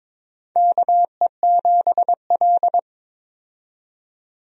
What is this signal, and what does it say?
Morse code, heard as a keyed tone, about 21 wpm, 700 Hz. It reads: KE7L
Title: USA Call Signs - 4 Characters - 22wpm